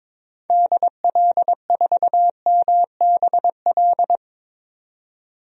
Morse code, heard as a keyed tone, about 21 wpm, 700 Hz. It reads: DL4MBL
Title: DXCC-100 Call Signs - 6 Characters - 22wpm